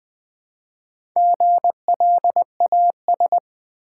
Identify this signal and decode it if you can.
Morse code, heard as a keyed tone, about 20 wpm, 700 Hz. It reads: GLAS
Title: Build A Word - 1 Letter At A Time - 3 to 5-Letter Words - 20wpm